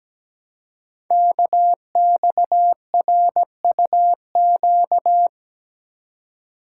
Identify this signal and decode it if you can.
Morse code, heard as a keyed tone, about 17 wpm, 700 Hz. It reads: KXRUQ
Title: Code Groups - 5 Letters - 17wpm